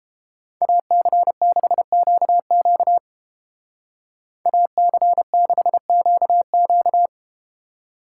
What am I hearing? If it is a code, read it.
Morse code, heard as a keyed tone, about 33 wpm, 700 Hz. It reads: AC6QQ AC6QQ